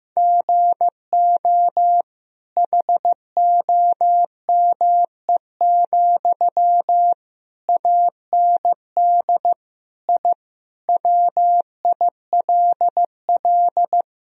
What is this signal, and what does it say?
Morse code, heard as a keyed tone, about 15 wpm, 700 Hz. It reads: GO HOME, AND I WILL
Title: Sentences from Top 1000 Words - Review Part 4 - 15wpm